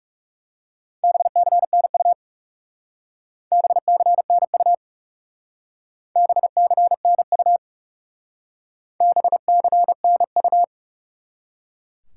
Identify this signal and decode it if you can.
Morse code, heard as a keyed tone, about 36 wpm, 700 Hz. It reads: BCNU BCNU BCNU BCNU
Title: Speed-Racing Top 180 QSO Elements 30wpm